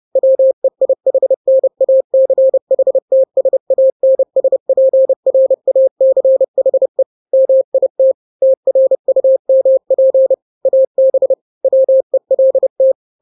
Morse code, heard as a keyed tone, about 30 wpm, 530 Hz. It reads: WEIHNACHTSANSPRACHE MIT TRUMP AB WELT
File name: 29wpm